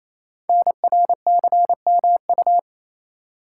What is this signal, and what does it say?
Morse code, heard as a keyed tone, about 29 wpm, 700 Hz. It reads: NRCMU
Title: Code Groups - 5 Letters - 28wpm